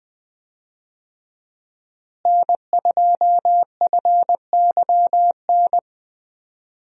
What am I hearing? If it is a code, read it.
Morse code, heard as a keyed tone, about 20 wpm, 700 Hz. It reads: N2FYN
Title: USA Call Signs - 5 Characters - Repeated 2x - 20wpm